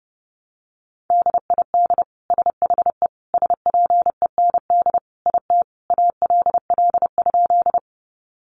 Morse code, heard as a keyed tone, about 30 wpm, 700 Hz. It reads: DID SHE SPEND IT ALL?
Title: Sentences from Top 800 Words 30wpm